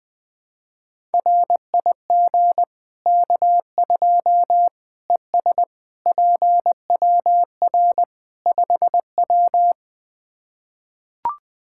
Morse code, heard as a keyed tone, about 21 wpm, 700 Hz. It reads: RIG K2 ES PWR 5W
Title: Practice QSO - Exchange 2 - Info - 20wpm